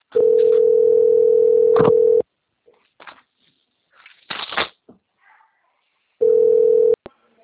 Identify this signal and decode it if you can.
Ringback tone (North America), 440 + 480 Hz.
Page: Maringa - PR